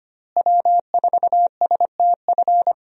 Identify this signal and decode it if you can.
Morse code, heard as a keyed tone, about 25 wpm, 700 Hz. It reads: W4STF
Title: USA Call Signs - 5 Characters - Repeated 2x - 25wpm